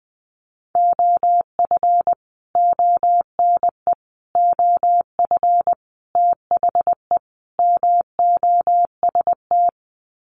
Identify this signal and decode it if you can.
Morse code, heard as a keyed tone, about 20 wpm, 700 Hz. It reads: OF ONE OF THE MOST
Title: Sets of 5 Words Encore2 20wpm